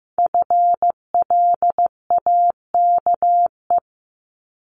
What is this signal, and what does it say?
Morse code, heard as a keyed tone, about 15 wpm, 700 Hz. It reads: FLAKE
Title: Multi-letter Phonemes 15wpm